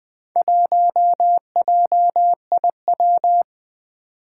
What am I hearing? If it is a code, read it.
Morse code, heard as a keyed tone, about 21 wpm, 700 Hz. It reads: 1JIW
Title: Four Challenging Letter-Number Combinations 20wpm